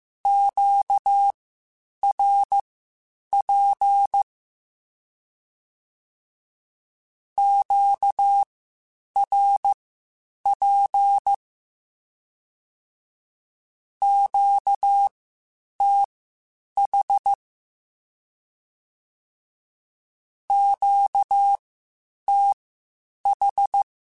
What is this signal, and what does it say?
Morse code, heard as a keyed tone, about 15 wpm, spaced as for 8 wpm, 800 Hz. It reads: QRP QRP QTH QTH
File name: (spaced) / Q-Calls Spaced-mp3s / Q Calls Twice sp @13